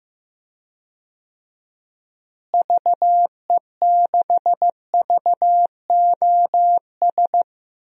Morse code, heard as a keyed tone, about 15 wpm, 700 Hz. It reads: VE6VOS
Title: DXCC-100 Call Signs - 6 Characters - Repeated 2x - 15wpm